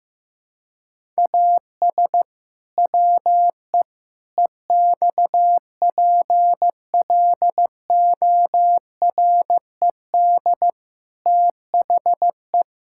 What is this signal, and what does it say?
Morse code, heard as a keyed tone, about 15 wpm, 700 Hz. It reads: AS WE EXPLORED THE
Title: Pangram Challenge v2 - 15wpm